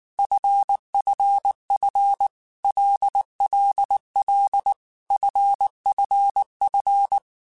Morse code, heard as a keyed tone, about 19 wpm, 800 Hz. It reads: FFF LLL FFF
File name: Groups@18 / Groups@18 mp3 / LFCYQ@18